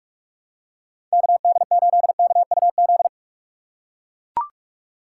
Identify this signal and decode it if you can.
Morse code, heard as a keyed tone, about 43 wpm, 700 Hz. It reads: KD8KUZ
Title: USA Call Signs - 6 Characters - 2x Character Spacing - 45wpm